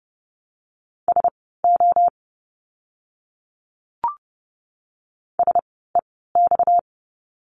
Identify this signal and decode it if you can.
Morse code, heard as a keyed tone, about 30 wpm, spaced as for 17 wpm, 700 Hz. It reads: SO SEX
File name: course / 30:10wpm / 230 Lesson 37f - Letter x - Words 30wpm